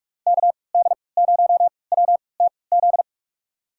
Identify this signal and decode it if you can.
Morse code, heard as a keyed tone, about 43 wpm, spaced as for 25 wpm, 700 Hz. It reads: KD0WTZ
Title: USA Call Signs - 6 Characters - 3x Character Spacing - 45wpm